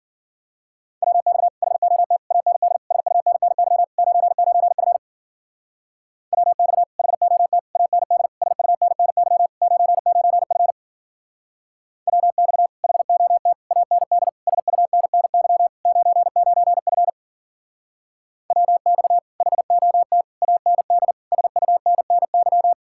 Morse code, heard as a keyed tone, about 48 wpm, 700 Hz. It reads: WX HOT AND SUNNY 99F WX HOT AND SUNNY 99F WX HOT AND SUNNY 99F WX HOT AND SUNNY
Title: Speed-Racing Common QSO Phrases Encore - 40wpm